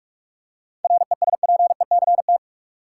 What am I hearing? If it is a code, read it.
Morse code, heard as a keyed tone, about 46 wpm, 700 Hz. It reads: RESPECT
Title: Seven Letter Words - 45wpm